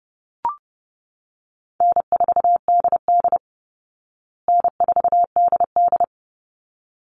Morse code, heard as a keyed tone, about 30 wpm, 700 Hz. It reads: N4DD N4DD
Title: Easy Call Signs Repeated 2x - 30wpm